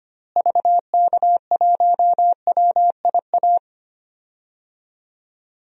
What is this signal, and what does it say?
Morse code, heard as a keyed tone, about 25 wpm, 700 Hz. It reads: VK1WIA
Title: DXCC-100 Call Signs - 6 Characters - Repeated 3x - 25wpm